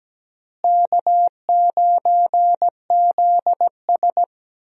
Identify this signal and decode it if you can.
Morse code, heard as a keyed tone, about 17 wpm, 700 Hz. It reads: K9ZS
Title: USA Call Signs - 4 Characters - Repeated 3x - 17wpm